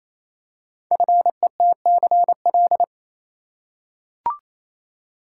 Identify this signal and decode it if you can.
Morse code, heard as a keyed tone, about 28 wpm, 700 Hz. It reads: FETCL
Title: Code Groups - 5 Letters - 28wpm